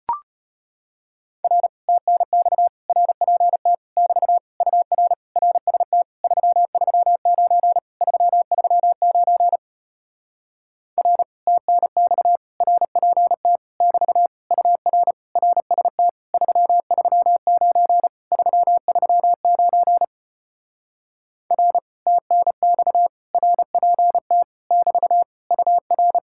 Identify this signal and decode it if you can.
Morse code, heard as a keyed tone, about 33 wpm, 700 Hz. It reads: R TNX RPT <BT> UR RST 339 339 R TNX RPT <BT> UR RST 339 339 R TNX RPT <BT> UR